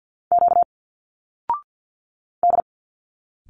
Morse code, heard as a keyed tone, about 46 wpm, 700 Hz. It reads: Q D
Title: Single Letters 50wpm